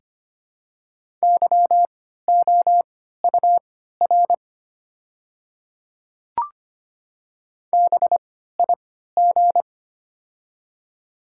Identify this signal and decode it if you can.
Morse code, heard as a keyed tone, about 25 wpm, spaced as for 14 wpm, 700 Hz. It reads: YOUR BIG
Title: 3x Character Spacing - Top 200 Words - 25wpm